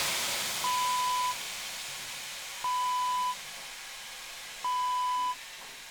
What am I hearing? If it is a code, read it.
There is no code.